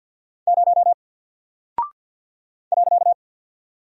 Morse code, heard as a keyed tone, about 48 wpm, 700 Hz. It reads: 0 .
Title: Single Character (Letter:Number:Punctuation:Prosign) Alt 50wpm